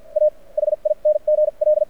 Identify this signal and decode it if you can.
Morse code, heard as a keyed tone, about 50 wpm, spaced as for 34 wpm, 600 Hz. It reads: AHING L
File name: archived / CRSnet